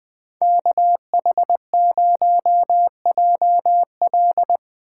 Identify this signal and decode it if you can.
Morse code, heard as a keyed tone, about 20 wpm, 700 Hz. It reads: KH0JL
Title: USA Call Signs - 5 Characters - 8x Character Spacing - 20wpm